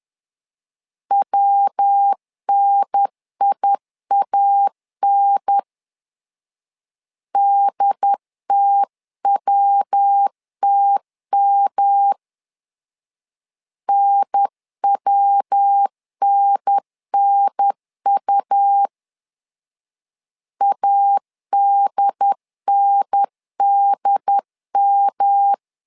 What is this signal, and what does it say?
Morse code, heard as a keyed tone, about 10 wpm, 800 Hz. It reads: WNIAN DTWTM NWNNU ADNDM
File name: LEZIONI / 3 MWRO / 3 MW 10wpm 1959